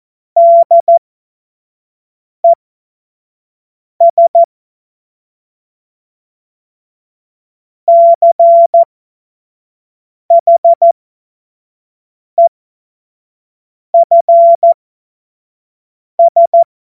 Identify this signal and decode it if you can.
Morse code, heard as a keyed tone, about 14 wpm, spaced as for 5 wpm, 680 Hz. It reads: DES CHEFS
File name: Morsekurs / Witze / Witz 13. Juni 2023 25 F70